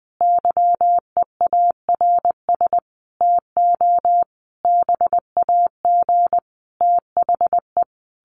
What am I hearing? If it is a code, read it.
Morse code, heard as a keyed tone, about 20 wpm, 700 Hz. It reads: YEARS TO BAG THE
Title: A Sound Of Thunder 20wpm